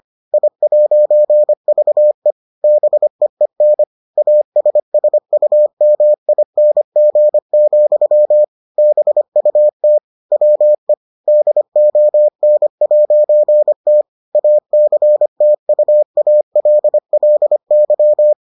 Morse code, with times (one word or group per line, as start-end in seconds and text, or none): 0.3-2.3 I'VE
2.6-3.8 BEEN
4.2-8.4 ASSUMING,
8.8-10.0 BUT
10.3-10.9 WE
11.3-14.0 DON'T
14.4-18.3 ACTUALLY